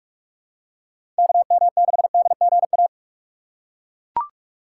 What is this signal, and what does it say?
Morse code, heard as a keyed tone, about 41 wpm, 700 Hz. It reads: KM6DGA
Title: USA Call Signs - 6 Characters - 2x Character Spacing - 45wpm